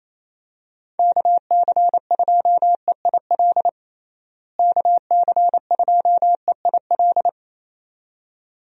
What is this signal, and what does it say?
Morse code, heard as a keyed tone, about 28 wpm, 700 Hz. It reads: KC2EIL KC2EIL
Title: USA Call Signs - 6 Characters - Repeated 2x - 28wpm